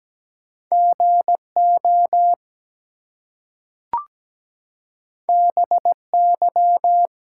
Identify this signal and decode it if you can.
Morse code, heard as a keyed tone, about 17 wpm, 700 Hz. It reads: GO BY
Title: Two Letter Words - 17wpm